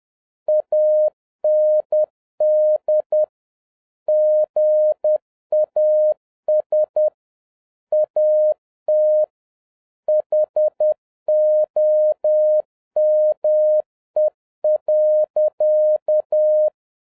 Morse code, heard as a keyed tone, about 10 wpm, 610 Hz. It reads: AND GAS AT HOME.
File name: 2011-07-mp3 / n36 THE 3.11 2 10wpm